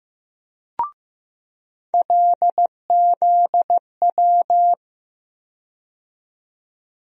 Morse code, heard as a keyed tone, about 15 wpm, 700 Hz. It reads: LZW
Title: USA Call Sign Postfixes - 3 Letters 15wpm